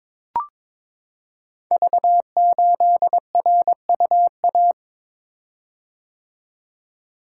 Morse code, heard as a keyed tone, about 22 wpm, 700 Hz. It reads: V8RUA
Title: Code Groups - 5 Letter:Numbers - 22wpm